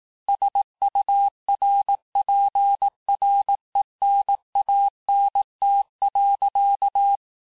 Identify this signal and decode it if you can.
Morse code, heard as a keyed tone, about 18 wpm, 800 Hz. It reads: SURPRENANT.
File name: cours-du-lundi files / cours-1594--FIN-de-QSO-vitesse-17mpm